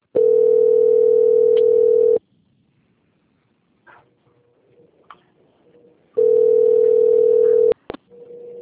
Ringback tone (North America), 440 + 480 Hz.
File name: out- - - - - .11719